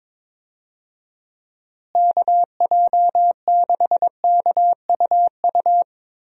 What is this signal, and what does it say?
Morse code, heard as a keyed tone, about 22 wpm, 700 Hz. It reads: KJ6KUU
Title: USA Call Signs - 6 Characters - Repeated 3x - 22wpm